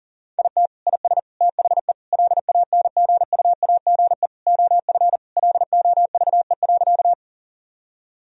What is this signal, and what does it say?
Morse code, heard as a keyed tone, about 40 wpm, 700 Hz. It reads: IT IS THE LANGUAGE OF LOVE.
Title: Sentences from Top 1000 Words - Review Part 3 - 40wpm